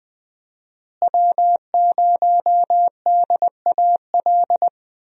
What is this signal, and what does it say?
Morse code, heard as a keyed tone, about 20 wpm, 700 Hz. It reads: W0DAL
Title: USA Call Signs - 5 Characters - Repeated 3x - 20wpm